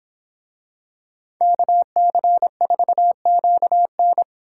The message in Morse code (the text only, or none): KC4QN